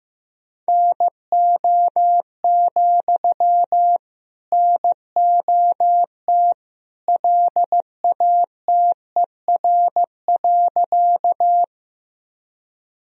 Morse code, heard as a keyed tone, about 15 wpm, 700 Hz. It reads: NO, NOT LATER.
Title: Sentences from Top 1000 Words - Review Part 4 - 15wpm